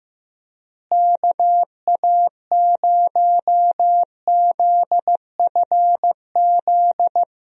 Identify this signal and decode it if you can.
Morse code, heard as a keyed tone, about 16 wpm, 700 Hz. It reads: KA0ZFZ
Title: USA Call Signs - 6 Characters - Repeated 2x - 15wpm